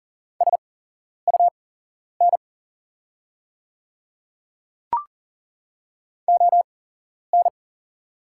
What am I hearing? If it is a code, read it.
Morse code, heard as a keyed tone, about 40 wpm, spaced as for 11 wpm, 690 Hz. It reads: SUN ON
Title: 8x Character Spacing - Top 300 Words - 40wpm